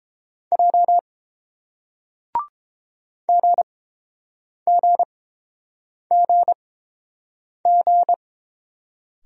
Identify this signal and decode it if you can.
Morse code, heard as a keyed tone, about 32 wpm, spaced as for 7 wpm, 690 Hz. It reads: J GGGG